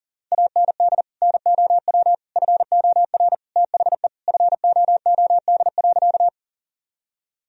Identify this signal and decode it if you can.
Morse code, heard as a keyed tone, about 40 wpm, 700 Hz. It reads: AND NOW FOR THE FOOD.
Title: Sentences from Top 1000 Words - Review Part 1 - 40wpm